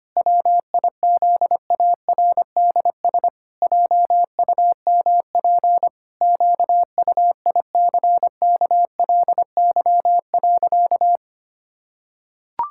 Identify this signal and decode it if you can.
Morse code, heard as a keyed tone, about 25 wpm, 700 Hz. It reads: WIZARDS JUMP QUICKLY.
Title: Pangram Challenge v2 - 25wpm